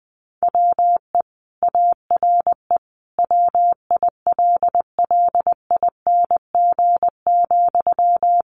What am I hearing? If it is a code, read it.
Morse code, heard as a keyed tone, about 20 wpm, 700 Hz. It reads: WE ARE WILLING,
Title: Sentences from Top 1300 Words 20wpm